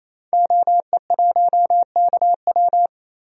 Morse code, heard as a keyed tone, about 28 wpm, 700 Hz. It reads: OE1KW
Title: DXCC-100 Call Signs - 5 Characters - 28wpm